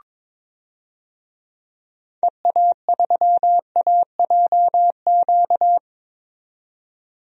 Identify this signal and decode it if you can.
Morse code, heard as a keyed tone, about 22 wpm, 700 Hz. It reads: EA3AJQ
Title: DXCC-100 Call Signs - 6 Characters - 22wpm